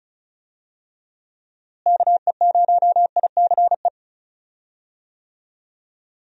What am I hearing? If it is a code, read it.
Morse code, heard as a keyed tone, about 36 wpm, 700 Hz. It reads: KE0ICE
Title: USA Call Signs - 6 Characters - Repeated 3x - 35wpm